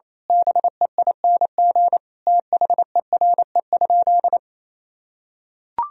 Morse code, heard as a keyed tone, about 28 wpm, 700 Hz. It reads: BEING THERE?
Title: Sentences from Top 1000 Words - Review Part 3 - 28wpm